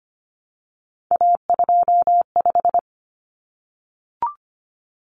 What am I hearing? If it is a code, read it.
Morse code, heard as a keyed tone, about 25 wpm, 700 Hz. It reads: A25
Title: Common Cut Numbers - 3-Digit 25wpm